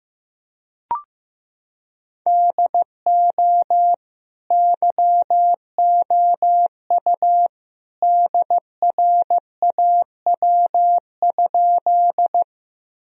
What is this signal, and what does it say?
Morse code, heard as a keyed tone, about 15 wpm, 700 Hz. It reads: DO YOU DRAW?
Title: Sentences from Top 1000 Words - Review Part 1 - 15wpm